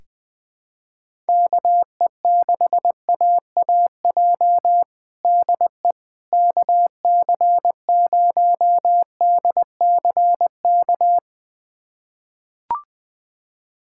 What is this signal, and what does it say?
Morse code, heard as a keyed tone, about 21 wpm, 700 Hz. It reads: KE6AAJ DE KC0DCK
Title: USA Callsign DE Callsign - 6 Characters - 20wpm